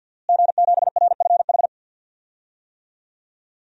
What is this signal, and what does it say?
Morse code, heard as a keyed tone, about 51 wpm, 700 Hz. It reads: G7RFH
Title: DXCC-100 Call Signs - 5 Characters - Repeated 2x - 50wpm